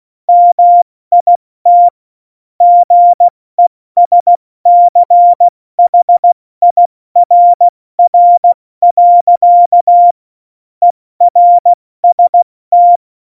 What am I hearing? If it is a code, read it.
Morse code, heard as a keyed tone, about 16 wpm, spaced as for 14 wpm, 700 Hz. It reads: MIT GESCHIRR. ERST
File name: Morsekurs / Witze / Witz 17.Dezember 2019, 70 F80